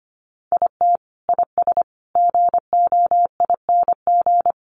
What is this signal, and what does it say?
Morse code, heard as a keyed tone, about 25 wpm, 700 Hz. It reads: IT IS GOING
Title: Sets of 5 Words 25wpm